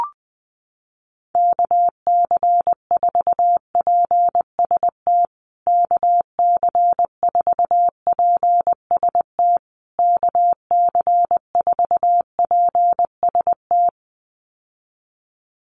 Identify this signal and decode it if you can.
Morse code, heard as a keyed tone, about 20 wpm, 700 Hz. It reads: KC4PST KC4PST KC4PST
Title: USA Call Signs Repeated 3x 20wpm